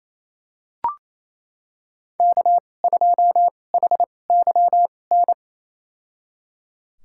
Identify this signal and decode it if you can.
Morse code, heard as a keyed tone, about 28 wpm, spaced as for 20 wpm, 700 Hz. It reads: K2HYN
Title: USA Call Signs - 5 Characters - 2x Character Spacing - 28wpm